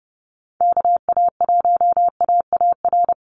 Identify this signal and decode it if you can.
Morse code, heard as a keyed tone, about 30 wpm, 700 Hz. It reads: KA1AAR
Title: USA Call Signs 30wpm